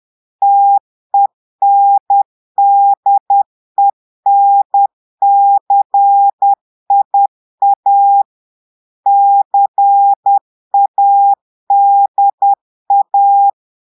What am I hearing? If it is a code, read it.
Morse code, heard as a keyed tone, about 10 wpm, 800 Hz. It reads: TENDENCIA CADA
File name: portugais texte en clair 2 vitesse 10 mots minutes